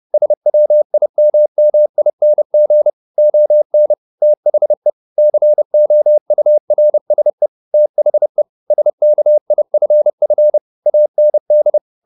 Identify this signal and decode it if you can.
Morse code, heard as a keyed tone, about 30 wpm, 600 Hz. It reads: SWIMMING ON THE COURSE THE SKIFF AND